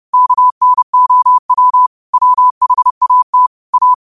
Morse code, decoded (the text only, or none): KNOW WHAT A